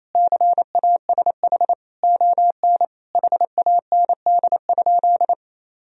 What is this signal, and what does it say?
Morse code, heard as a keyed tone, about 28 wpm, 700 Hz. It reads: CASH ON HAND?
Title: Sentences from Top 1000 Words - Review Part 4 - 28wpm